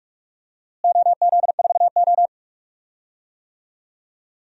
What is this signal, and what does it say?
Morse code, heard as a keyed tone, about 45 wpm, 700 Hz. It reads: OZ4O
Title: DXCC-100 Call Signs - 4 Characters - Repeated 2x - 45wpm